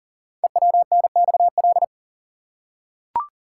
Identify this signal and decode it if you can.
Morse code, heard as a keyed tone, about 42 wpm, 700 Hz. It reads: EWNXL
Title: Code Groups - 5 Letters - 40wpm